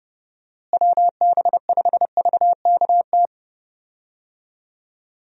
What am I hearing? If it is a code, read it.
Morse code, heard as a keyed tone, about 30 wpm, 700 Hz. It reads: WB5VKT
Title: USA Call Signs - 6 Characters - Repeated 3x - 30wpm